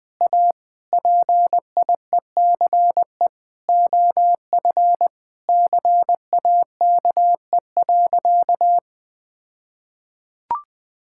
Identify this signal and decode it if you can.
Morse code, heard as a keyed tone, about 20 wpm, 700 Hz. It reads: A PIECE OF CAKE.
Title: All-Age Appropriate Jokes - 20wpm